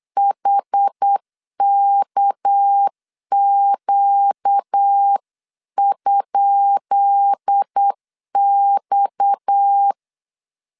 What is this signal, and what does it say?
Morse code, heard as a keyed tone, about 8 wpm, 800 Hz. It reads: HKQ?X
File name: LEZIONI / 10 SIGNS / 10B INT 8wpm 3534